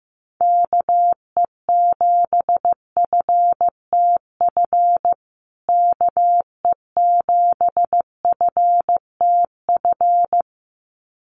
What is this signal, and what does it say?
Morse code, heard as a keyed tone, about 15 wpm, 700 Hz. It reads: KE7FTF KE7FTF
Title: USA Call Signs Repeated 2x 15wpm